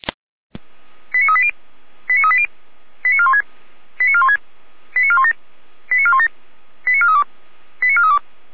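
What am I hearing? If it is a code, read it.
Five-tone selective call (ZVEI), heard as quick runs of five tones, tones of about 70 ms. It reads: 89380 89380 89416 89416 89417 89417 89432 89432
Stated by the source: Five-tone ZVEI selective call 89402, received by SDR.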